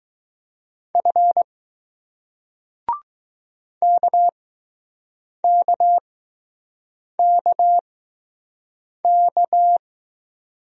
Morse code, heard as a keyed tone, about 20 wpm, spaced as for 6 wpm, 700 Hz. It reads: F KKKK